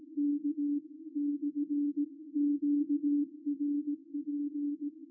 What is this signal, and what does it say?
Morse code, heard as a keyed tone, about 18 wpm, 290 Hz. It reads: K /QRP
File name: r2auk-g3xjs-qrp-qso-recorded-by-ruud-vos